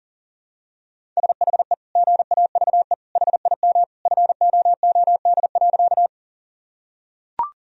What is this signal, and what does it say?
Morse code, heard as a keyed tone, about 40 wpm, 700 Hz. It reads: SHE GAVE HIM FOOD.
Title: Sentences from Top 1000 Words - Review Part 1 - 40wpm